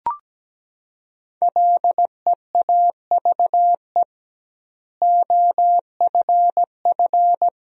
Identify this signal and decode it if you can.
Morse code, heard as a keyed tone, about 17 wpm, 700 Hz. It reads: LEAVE OFF
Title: 2x Word-Spacing - Sentences from Top 400 Words - 17wpm